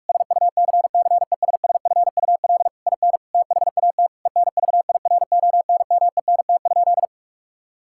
Morse code, heard as a keyed tone, about 42 wpm, 700 Hz. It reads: SUCCESSFUL IN THAT ENVIRONMENT?
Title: Sentences from Top 1000 Words - Review Part 5 - 45wpm